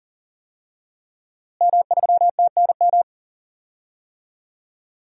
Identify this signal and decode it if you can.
Morse code, heard as a keyed tone, about 39 wpm, 700 Hz. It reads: M3TNM
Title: DXCC-100 Call Signs - 5 Characters - Repeated 2x - 40wpm